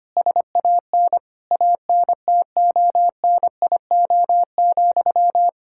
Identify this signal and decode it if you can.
Morse code, heard as a keyed tone, about 25 wpm, 700 Hz. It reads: SAN ANTONIO,
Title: K1USN SST QSO Practice 25wpm